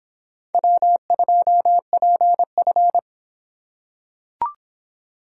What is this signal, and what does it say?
Morse code, heard as a keyed tone, about 26 wpm, 700 Hz. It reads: W2PF